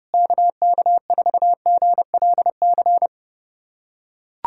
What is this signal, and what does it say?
Morse code, heard as a keyed tone, about 32 wpm, 700 Hz. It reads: KK4GLC
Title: USA Callsign DE Callsign 30wpm